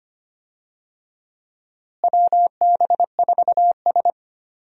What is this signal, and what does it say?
Morse code, heard as a keyed tone, about 25 wpm, 700 Hz. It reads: WB4S
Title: USA Call Signs - 4 Characters - Repeated 2x - 25wpm